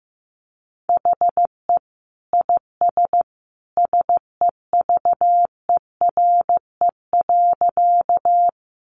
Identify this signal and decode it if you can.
Morse code, heard as a keyed tone, about 15 wpm, 700 Hz. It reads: HE IS SEVERE.
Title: Sentences from Top 1400 Words 15wpm